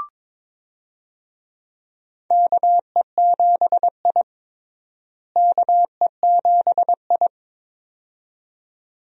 Morse code, heard as a keyed tone, about 21 wpm, 700 Hz. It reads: KE7I KE7I
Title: USA Call Signs - 4 Characters - Repeated 2x - 22wpm